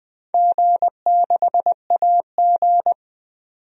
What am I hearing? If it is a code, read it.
Morse code, heard as a keyed tone, about 20 wpm, 700 Hz. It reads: G6AG
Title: DXCC-100 Call Signs - 4 Characters - 20wpm